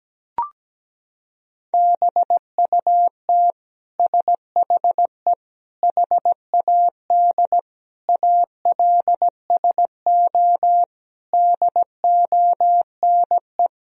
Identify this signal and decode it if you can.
Morse code, heard as a keyed tone, about 17 wpm, 700 Hz. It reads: BUT SHE HAD ALSO DONE
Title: Sentences from Top 1000 Words - Review Part 1 - 17wpm